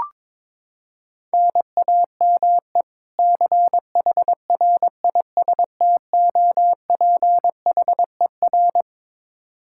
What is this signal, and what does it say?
Morse code, heard as a keyed tone, about 22 wpm, 700 Hz. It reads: NAME CHRISTOPHER
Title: Practice QSO - Exchange 1 - Beginning - 22wpm